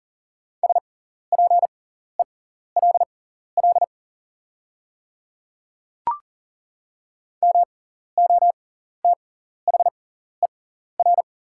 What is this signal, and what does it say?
Morse code, heard as a keyed tone, about 40 wpm, spaced as for 14 wpm, 690 Hz. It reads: SPELL MOTHER
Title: 6x Character Spacing - Top 200 Words - 40wpm